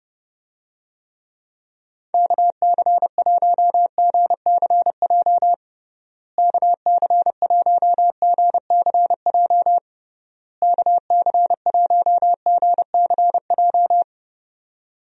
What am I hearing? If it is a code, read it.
Morse code, heard as a keyed tone, about 30 wpm, 700 Hz. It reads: KC1GCJ KC1GCJ KC1GCJ
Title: USA Call Signs - 6 Characters - Repeated 3x - 30wpm